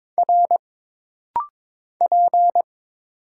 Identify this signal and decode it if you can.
Morse code, heard as a keyed tone, about 22 wpm, 700 Hz. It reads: R P
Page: This rapid-fire timing is geared to push your ICR (Instant Character Recognition) skills.